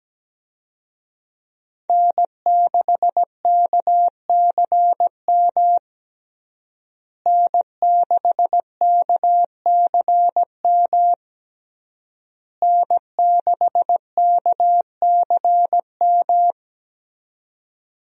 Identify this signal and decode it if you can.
Morse code, heard as a keyed tone, about 17 wpm, 700 Hz. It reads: N6KCM N6KCM N6KCM
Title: USA Call Signs - 5 Characters - Repeated 3x - 17wpm